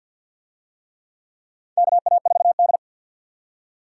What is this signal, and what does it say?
Morse code, heard as a keyed tone, about 53 wpm, 700 Hz. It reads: KA4D
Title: USA Call Signs - 4 Characters - 50wpm